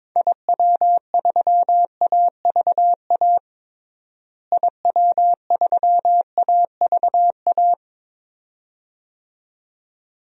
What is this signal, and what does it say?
Morse code, heard as a keyed tone, about 21 wpm, 700 Hz. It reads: IW3AVA IW3AVA
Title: DXCC-100 Call Signs - 6 Characters - Repeated 2x - 22wpm